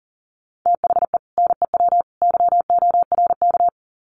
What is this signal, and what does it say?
Morse code, heard as a keyed tone, about 40 wpm, 700 Hz. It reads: THE NEW YORK
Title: Sets of 3 Words 40wpm